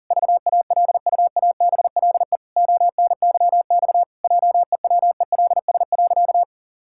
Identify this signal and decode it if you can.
Morse code, heard as a keyed tone, about 40 wpm, 700 Hz. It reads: VALUABLE ONYX JEWELS.
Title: Pangram Challenge v2 - 40wpm